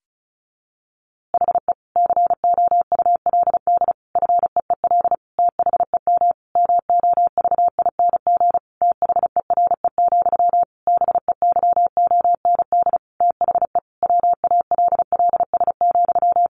Morse code, heard as a keyed tone, about 34 wpm, 700 Hz. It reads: HE COULD FEEL THEM MOVING THERE, BEYOND THE WALLS,
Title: A Sound Of Thunder 35wpm